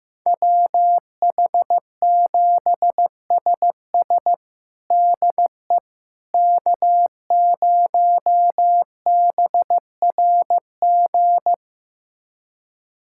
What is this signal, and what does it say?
Morse code, heard as a keyed tone, about 15 wpm, 700 Hz. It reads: WH7SS DE K0BRG
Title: USA Callsign DE Callsign - 5 Characters - 15wpm